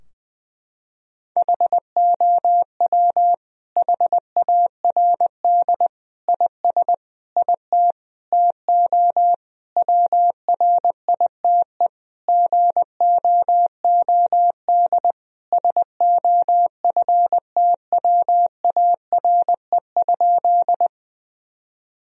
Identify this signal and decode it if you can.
Morse code, heard as a keyed tone, about 21 wpm, 700 Hz. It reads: HOW HARD IS IT TO WRITE GOOD SOFTWARE?
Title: Sentences from Top 1000 Words - Review Part 3 - 20wpm